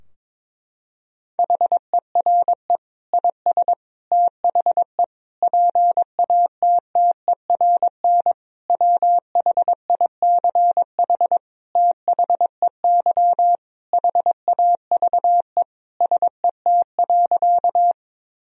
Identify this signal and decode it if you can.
Morse code, heard as a keyed tone, about 22 wpm, 700 Hz. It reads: HERE IS THE PATTERN WHICH THEY HAVE SET.
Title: Sentences from Top 1000 Words - Review Part 2 - 22wpm